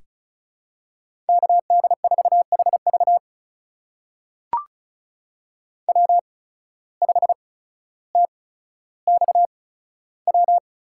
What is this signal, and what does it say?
Morse code, heard as a keyed tone, about 36 wpm, spaced as for 15 wpm, 700 Hz. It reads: KD4HV W5TXW
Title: USA Call Signs - 5 Characters - 8x Character Spacing - 35wpm